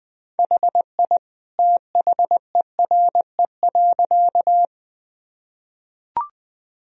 Morse code, heard as a keyed tone, about 21 wpm, 700 Hz. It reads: HI THERE.
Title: Sentences from Top 1000 Words - Review Part 3 - 20wpm